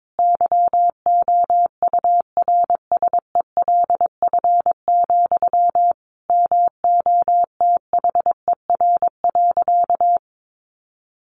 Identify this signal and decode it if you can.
Morse code, heard as a keyed tone, about 22 wpm, 700 Hz. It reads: YOURSELF, MOTHER.
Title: Sentences from Top 1300 Words 22wpm